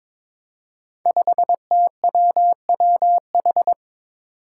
Morse code, heard as a keyed tone, about 22 wpm, 700 Hz. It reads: 5TWWH
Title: Five Challenging Letter-Number Combinations 22wpm